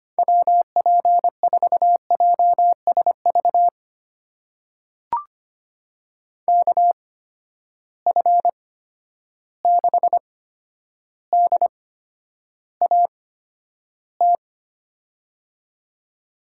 Morse code, heard as a keyed tone, about 25 wpm, spaced as for 11 wpm, 700 Hz. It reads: WP4JSV KF6DAT
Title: USA Call Signs - 6 Characters - 8x Character Spacing - 25wpm